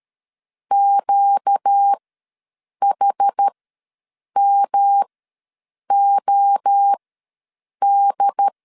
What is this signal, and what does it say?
Morse code, heard as a keyed tone, about 13 wpm, spaced as for 7 wpm, 800 Hz. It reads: QHMOD
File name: LEZIONI / 5 LBVC / 5 L ws 12wpm 2958